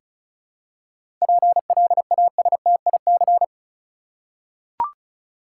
Morse code, heard as a keyed tone, about 34 wpm, 700 Hz. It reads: PLASTIC
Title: Seven Letter Words - 35wpm